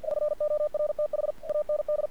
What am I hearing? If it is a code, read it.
Morse code, heard as a keyed tone, about 46 wpm, 600 Hz. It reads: <SN>ORTH AND
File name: archived / CRSnet